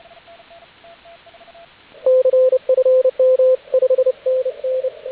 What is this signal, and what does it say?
Morse code, heard as a keyed tone, about 28 wpm, 520 Hz, with radio noise. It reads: CFM 5NN
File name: AUDIO2019 / HG8DX CW